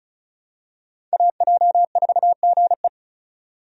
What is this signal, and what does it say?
Morse code, heard as a keyed tone, about 36 wpm, 700 Hz. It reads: AJ4GE
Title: USA Call Signs - 5 Characters - 35wpm